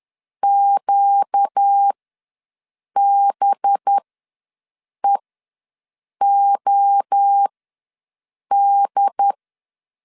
Morse code, heard as a keyed tone, about 10 wpm, spaced as for 6 wpm, 800 Hz. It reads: QBEOD
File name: LEZIONI / 5 LBVC / 5 LBVC ws 10wpm 3733